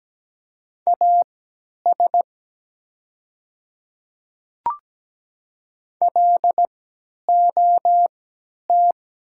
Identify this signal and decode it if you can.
Morse code, heard as a keyed tone, about 17 wpm, spaced as for 10 wpm, 700 Hz. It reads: AS LOT
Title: 3x Character Spacing - Top 500 Words - 17wpm